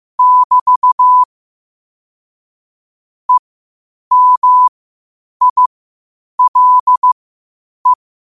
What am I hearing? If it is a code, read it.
Morse code, heard as a keyed tone, about 15 wpm, spaced as for 8 wpm, 1000 Hz. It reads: <BT> EMILE
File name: qotdfiles / 8wpm